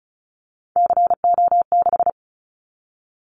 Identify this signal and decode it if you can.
Morse code, heard as a keyed tone, about 34 wpm, 700 Hz. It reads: CO6
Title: Callsign Prefixes for Top 100 DXCC - 35wpm